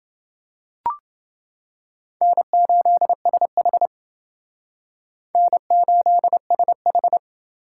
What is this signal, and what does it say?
Morse code, heard as a keyed tone, about 29 wpm, 700 Hz. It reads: N8SH N8SH